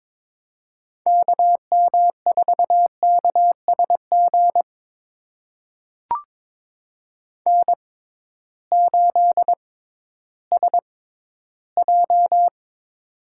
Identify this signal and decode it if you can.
Morse code, heard as a keyed tone, about 22 wpm, spaced as for 13 wpm, 700 Hz. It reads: KM4KSG N8SJ
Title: USA Call Signs - 6x Character Spacing - 22wpm